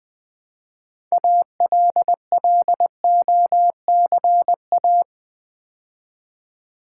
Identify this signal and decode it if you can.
Morse code, heard as a keyed tone, about 20 wpm, 700 Hz. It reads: ALLOCA
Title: Build A Word - 2 Letters At A Time - 8-Letter Words - 20wpm